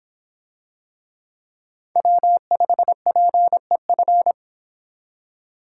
Morse code, heard as a keyed tone, about 25 wpm, 700 Hz. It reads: W5PEF